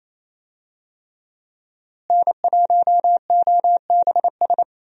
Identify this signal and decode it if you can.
Morse code, heard as a keyed tone, about 28 wpm, 700 Hz. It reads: N1OBS
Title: USA Call Signs - 5 Characters - Repeated 3x - 28wpm